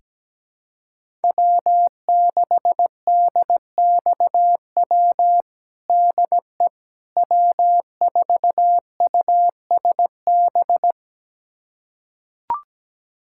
Morse code, heard as a keyed tone, about 17 wpm, 700 Hz. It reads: W6DXW DE W4USB
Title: USA Callsign DE Callsign - 5 Characters - 17wpm